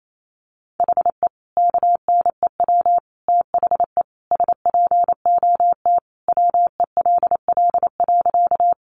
Morse code, heard as a keyed tone, about 28 wpm, 700 Hz. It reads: HE KNEW THE SPOT WELL.
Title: Sentences from Top 1100 Words 28wpm